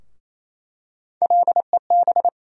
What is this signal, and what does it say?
Morse code, heard as a keyed tone, about 27 wpm, 700 Hz. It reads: LEB
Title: USA Call Sign Postfixes - 3 Letters 28wpm